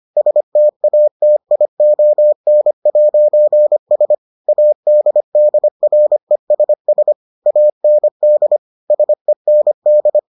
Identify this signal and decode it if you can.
Morse code, heard as a keyed tone, about 25 wpm, 600 Hz. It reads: STATION'S ADDRESS AND SEND